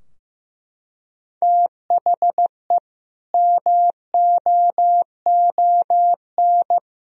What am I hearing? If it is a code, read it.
Morse code, heard as a keyed tone, about 15 wpm, 700 Hz. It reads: THE MOON
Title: Sentences from Top 1000 Words - Review Part 3 - 15wpm